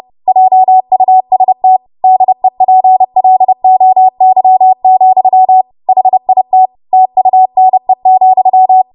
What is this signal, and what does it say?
Morse code, heard as a keyed tone, about 30 wpm, 750 Hz. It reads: JUST DEPLOY, HIT TUNE,
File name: Morse / 30WPM